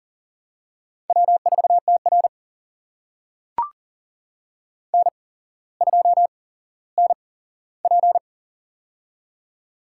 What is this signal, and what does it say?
Morse code, heard as a keyed tone, about 40 wpm, spaced as for 17 wpm, 700 Hz. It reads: W4TR N2NP
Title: USA Call Signs - 4 Characters - 8x Character Spacing - 40wpm